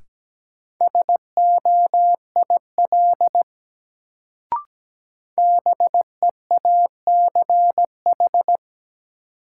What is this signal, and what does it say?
Morse code, heard as a keyed tone, about 17 wpm, 700 Hz. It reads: SOIL BEACH
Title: Sentences from Top 1500 Words - New Word Drill Alt - 17wpm